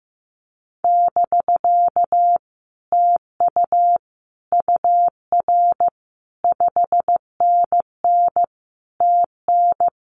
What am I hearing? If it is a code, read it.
Morse code, heard as a keyed tone, about 15 wpm, 700 Hz. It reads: <BK> TU UR 5NN TN
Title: Common QSO Phrases Encore 15wpm